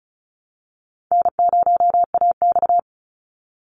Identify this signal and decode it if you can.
Morse code, heard as a keyed tone, about 34 wpm, 700 Hz. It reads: N0AX
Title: Easy Call Signs Repeated 2x - 35wpm